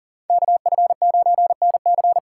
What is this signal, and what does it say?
Morse code, heard as a keyed tone, about 40 wpm, 700 Hz. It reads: KF9NC
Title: USA Call Signs - 5 Characters - 2x Character Spacing - 40wpm